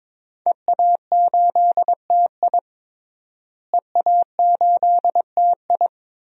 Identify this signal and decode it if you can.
Morse code, heard as a keyed tone, about 23 wpm, 700 Hz. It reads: EA8TI EA8TI
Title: DXCC-100 Call Signs - 5 Characters - Repeated 2x - 22wpm